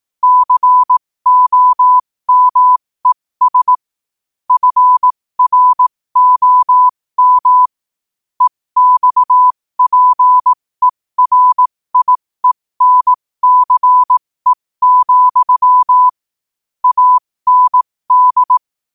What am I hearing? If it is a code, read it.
Morse code, heard as a keyed tone, about 18 wpm, spaced as for 15 wpm, 1000 Hz. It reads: COMES FROM EXPERIENCE, AND
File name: qotdfiles / 15wpm